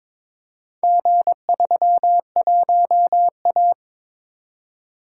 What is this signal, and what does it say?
Morse code, heard as a keyed tone, about 23 wpm, 700 Hz. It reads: G31A
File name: practice / 22wpm / Four Letter-Number Combinations - 22wpm